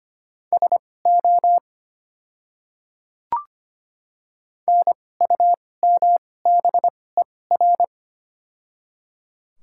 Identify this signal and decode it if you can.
Morse code, heard as a keyed tone, about 25 wpm, spaced as for 18 wpm, 690 Hz. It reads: SO NUMBER
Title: 2x Character Spacing - Top 100 Words - 25wpm